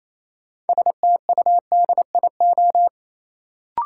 Morse code, heard as a keyed tone, about 27 wpm, 700 Hz. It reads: STUDIO
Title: Sentences from Top 1200 Words - New Word Drill Alt - 28wpm